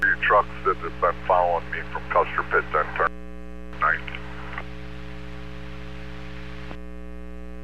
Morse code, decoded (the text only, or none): none